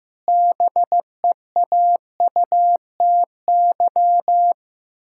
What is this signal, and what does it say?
Morse code, heard as a keyed tone, about 15 wpm, 700 Hz. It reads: BEAUTY
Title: Pangram Challenge v2 - 15wpm